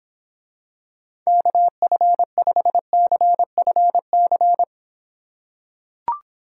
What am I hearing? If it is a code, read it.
Morse code, heard as a keyed tone, about 26 wpm, 700 Hz. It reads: KF5CFC